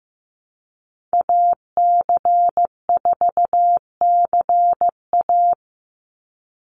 Morse code, heard as a keyed tone, about 15 wpm, 700 Hz. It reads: AC4CA
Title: Call Signs Repeated 3x - 15wpm